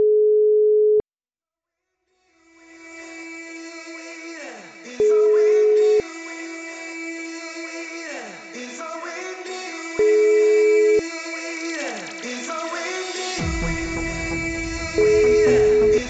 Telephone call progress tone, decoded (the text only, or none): ringback tone